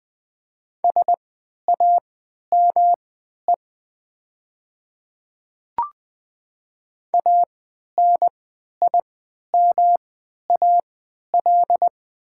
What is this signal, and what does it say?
Morse code, heard as a keyed tone, about 20 wpm, spaced as for 11 wpm, 690 Hz. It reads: SAME ANIMAL
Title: 3x Character Spacing - Top 200 Words - 20wpm